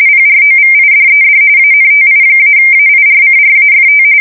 RTTY, about 45.45 baud. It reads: PROVIDED ANTI-AIRCRAFT
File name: fsk170 / USS-New-Jersey